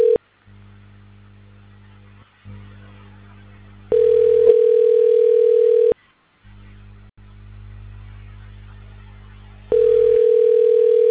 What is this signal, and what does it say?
Ringback tone (North America), 440 + 480 Hz.